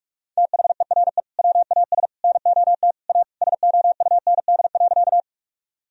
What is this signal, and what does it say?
Morse code, heard as a keyed tone, about 46 wpm, 700 Hz. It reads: THERE WAS NOT A SOUND.
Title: Sentences from Top 1000 Words - Review Part 1 - 45wpm